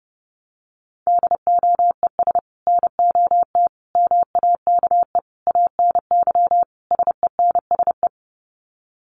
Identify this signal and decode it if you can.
Morse code, heard as a keyed tone, about 30 wpm, 700 Hz. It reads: DOES NOT MAKE ANY SENSE
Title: Sets of 5 Words Encore2 30wpm